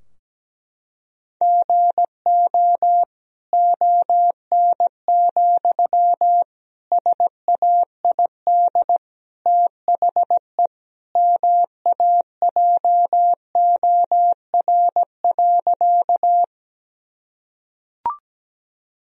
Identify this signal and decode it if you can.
Morse code, heard as a keyed tone, about 17 wpm, 700 Hz. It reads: GO ON, SAID THE MAJOR.
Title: Sentences from Top 1000 Words - Review Part 4 - 17wpm